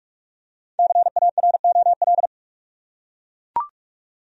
Morse code, heard as a keyed tone, about 43 wpm, 700 Hz. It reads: CAROL
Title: USA Common Names - Max 5 Letters - 45wpm